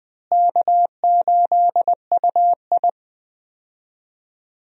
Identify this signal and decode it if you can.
Morse code, heard as a keyed tone, about 19 wpm, 700 Hz. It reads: K8UI
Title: USA Call Signs - 4 Characters - Repeated 3x - 20wpm